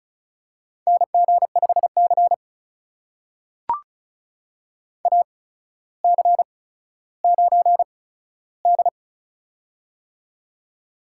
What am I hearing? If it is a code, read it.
Morse code, heard as a keyed tone, about 36 wpm, spaced as for 15 wpm, 700 Hz. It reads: NG5C AC9D
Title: USA Call Signs - 4 Characters - 8x Character Spacing - 35wpm